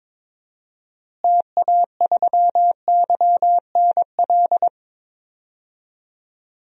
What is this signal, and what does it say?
Morse code, heard as a keyed tone, about 21 wpm, 700 Hz. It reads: TA3YNL
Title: DXCC-100 Call Signs - 6 Characters - Repeated 2x - 22wpm